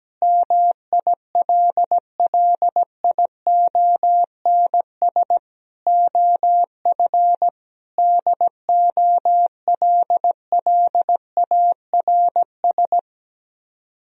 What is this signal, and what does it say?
Morse code, heard as a keyed tone, about 17 wpm, 700 Hz. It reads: MILLIONS OF DOLLARS
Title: Speed-Racing - Sets of 5 Words 17wpm